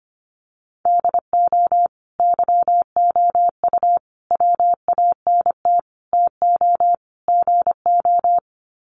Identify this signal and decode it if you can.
Morse code, heard as a keyed tone, about 25 wpm, 700 Hz. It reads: DO YOU WANT TO GO
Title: Sets of 5 Words Encore2 25wpm